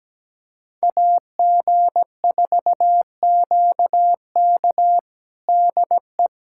Morse code, heard as a keyed tone, about 17 wpm, 700 Hz. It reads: AG4QK DE
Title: USA Callsign DE Callsign - 5 Characters - 17wpm